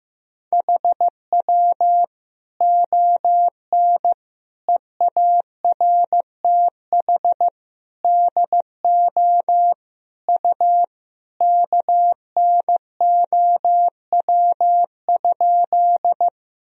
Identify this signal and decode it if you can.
Morse code, heard as a keyed tone, about 15 wpm, 700 Hz. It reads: HW ON EARTH DO U KNOW?
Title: Sentences from Top 200 Words and Abbreviations 15wpm